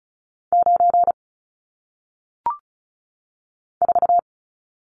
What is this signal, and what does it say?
Morse code, heard as a keyed tone, about 36 wpm, 700 Hz. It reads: 9 4
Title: Numbers - 1-Digit 35wpm